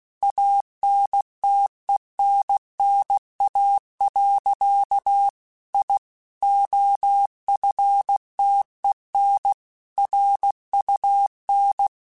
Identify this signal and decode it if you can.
Morse code, heard as a keyed tone, about 16 wpm, 800 Hz. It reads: ANTENNA. I OFTEN RUN
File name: QSOs / QSOs-mp3s / WA7QOX 2@15